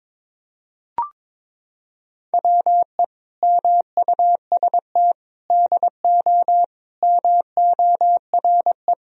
Morse code, heard as a keyed tone, about 22 wpm, 700 Hz. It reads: WE MUST DO MORE
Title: Sentences from Top 1000 Words - Review Part 3 - 22wpm